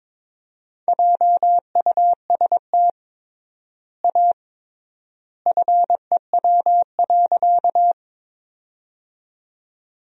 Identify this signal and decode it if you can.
Morse code, heard as a keyed tone, about 22 wpm, 700 Hz. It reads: JUST A FEW.
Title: 3x Word-Spacing - Sentences from Top 300 Words - 22wpm